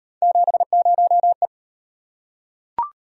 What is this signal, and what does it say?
Morse code, heard as a keyed tone, about 36 wpm, 700 Hz. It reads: 70E